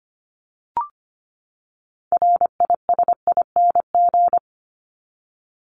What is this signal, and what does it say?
Morse code, heard as a keyed tone, about 25 wpm, 700 Hz. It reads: RISING
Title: ING Suffix - Top 100 Words with Base Words 25wpm